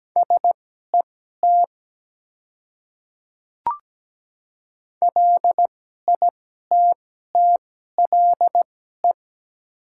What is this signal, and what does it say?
Morse code, heard as a keyed tone, about 17 wpm, spaced as for 12 wpm, 700 Hz. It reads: SET LITTLE
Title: 2x Character Spacing - Top 200 Words - 17wpm